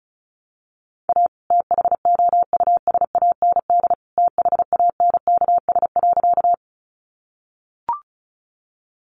Morse code, heard as a keyed tone, about 34 wpm, 700 Hz. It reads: A THOUSAND THANKS.
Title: Sentences from Top 600 Words 35wpm